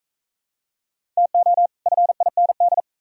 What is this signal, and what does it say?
Morse code, heard as a keyed tone, about 38 wpm, 700 Hz. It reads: TO FIND